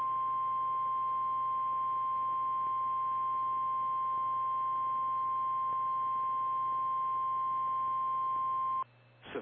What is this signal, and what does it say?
Emergency Alert System (SAME) tones (the 1050 Hz attention tone). It is heard from 0.6 to 8.5 s.
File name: eas / KKPRKXPN / KKPRKXPN